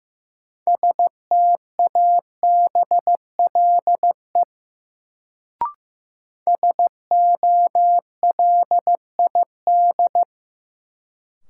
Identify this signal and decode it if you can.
Morse code, heard as a keyed tone, about 15 wpm, 700 Hz. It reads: STABLE SOLID
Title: Sentences from Top 1500 Words - New Word Drill - 15wpm